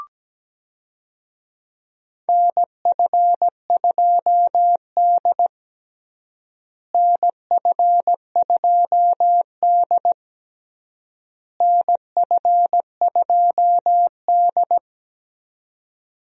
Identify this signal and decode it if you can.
Morse code, heard as a keyed tone, about 17 wpm, 700 Hz. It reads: NF2D NF2D NF2D
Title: USA Call Signs - 4 Characters - Repeated 3x - 17wpm